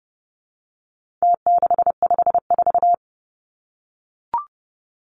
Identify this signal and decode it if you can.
Morse code, heard as a keyed tone, about 30 wpm, 700 Hz. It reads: T654
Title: Numbers - Contest Style - 4-Digit 30wpm